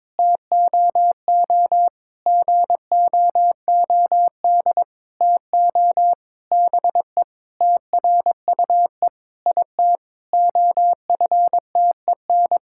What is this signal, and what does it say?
Morse code, heard as a keyed tone, about 22 wpm, 700 Hz. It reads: TOO GOOD TO BE TRUE IT OFTEN
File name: Sentences with 9-Word Phrases - 22wpm